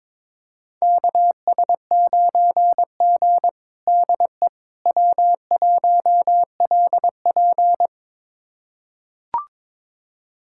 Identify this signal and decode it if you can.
Morse code, heard as a keyed tone, about 22 wpm, 700 Hz. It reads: KS9G DE W1LP
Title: USA Callsign DE Callsign - 4 Characters - 22wpm